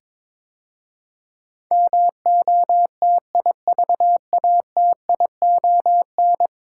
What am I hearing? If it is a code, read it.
Morse code, heard as a keyed tone, about 22 wpm, 700 Hz. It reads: MOTIVATION
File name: practice / 22wpm / 10 Letter Words - First 3 Letters Given - 22wpm